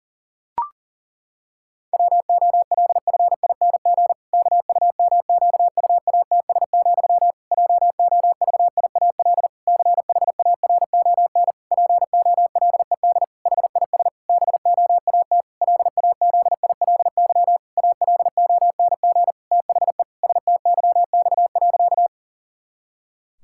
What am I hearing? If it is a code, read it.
Morse code, heard as a keyed tone, about 40 wpm, 700 Hz. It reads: WOLFING KUMQUATS, JOVIAL CHARON POLED HIS BOAT LAZILY ALONG THE STYX.
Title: Pangram Challenge v2 - 40wpm